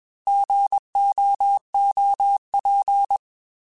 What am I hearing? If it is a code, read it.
Morse code, heard as a keyed tone, about 21 wpm, 800 Hz. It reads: GOOP
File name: Alphabetical / Alphabetical mp3s / Gs @20